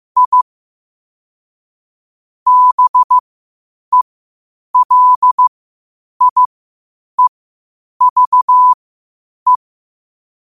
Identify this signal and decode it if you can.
Morse code, heard as a keyed tone, about 15 wpm, spaced as for 8 wpm, 1000 Hz. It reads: I BELIEVE
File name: qotdfiles / 8wpm